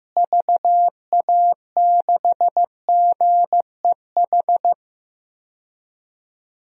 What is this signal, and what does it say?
Morse code, heard as a keyed tone, about 15 wpm, 700 Hz. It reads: VA6GEH
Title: DXCC-100 Call Signs - 6 Characters - Repeated 3x - 15wpm